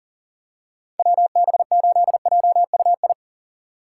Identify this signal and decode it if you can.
Morse code, heard as a keyed tone, about 40 wpm, 700 Hz. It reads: WB8JUI
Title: K1USN SST QSO Practice - 40wpm